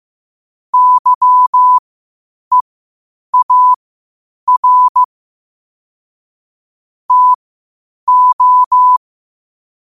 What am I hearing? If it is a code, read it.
Morse code, heard as a keyed tone, about 15 wpm, spaced as for 8 wpm, 1000 Hz. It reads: YEAR TO